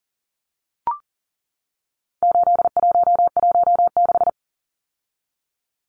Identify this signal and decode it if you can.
Morse code, heard as a keyed tone, about 40 wpm, 700 Hz. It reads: 8116
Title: Numbers - 4-Digit 40wpm